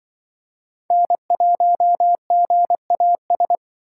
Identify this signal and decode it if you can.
Morse code, heard as a keyed tone, about 25 wpm, 700 Hz. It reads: N1GAS